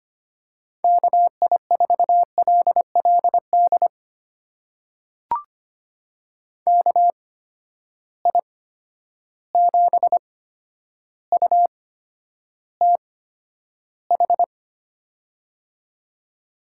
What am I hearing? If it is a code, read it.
Morse code, heard as a keyed tone, about 25 wpm, spaced as for 11 wpm, 700 Hz. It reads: KI4LLD KI7UTH
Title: USA Call Signs - 6 Characters - 8x Character Spacing - 25wpm